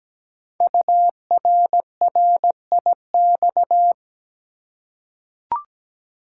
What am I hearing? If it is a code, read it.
Morse code, heard as a keyed tone, about 17 wpm, 700 Hz. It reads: URRIX
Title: Code Groups - 5 Letters - 17wpm